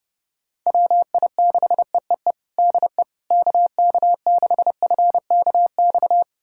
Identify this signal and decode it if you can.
Morse code, heard as a keyed tone, about 30 wpm, 700 Hz. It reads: WI6EEE DE KK6FKX
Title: USA Callsign DE Callsign - 6 Characters - 30wpm